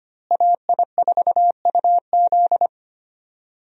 Morse code, heard as a keyed tone, about 25 wpm, 700 Hz. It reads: AI4UZ
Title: USA Call Signs - 5 Characters - Repeated 3x - 25wpm